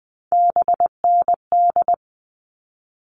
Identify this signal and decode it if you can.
Morse code, heard as a keyed tone, about 20 wpm, 700 Hz. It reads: BND
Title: All Cut Numbers - 3-Digit 20wpm